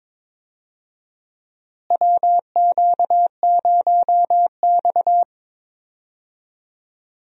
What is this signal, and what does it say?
Morse code, heard as a keyed tone, about 22 wpm, 700 Hz. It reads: WQ0X
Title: USA Call Signs - 4 Characters - Repeated 3x - 22wpm